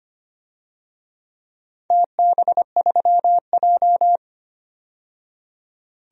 Morse code, heard as a keyed tone, about 25 wpm, 700 Hz. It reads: TB3J
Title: DXCC-100 Call Signs - 4 Characters - 25wpm